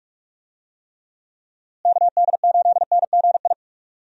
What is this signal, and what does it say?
Morse code, heard as a keyed tone, about 46 wpm, 700 Hz. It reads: KD8NGI
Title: USA Call Signs - 6 Characters - Repeated 2x - 45wpm